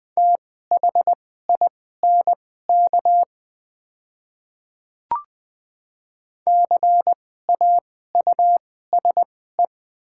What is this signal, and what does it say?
Morse code, heard as a keyed tone, about 20 wpm, spaced as for 15 wpm, 700 Hz. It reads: THINK CAUSE
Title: 2x Character Spacing - Top 200 Words - 20wpm